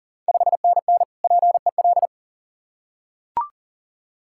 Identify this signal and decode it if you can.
Morse code, heard as a keyed tone, about 40 wpm, 690 Hz. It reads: 5NN PEL
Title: Ontario QSO Party Exchange 40wpm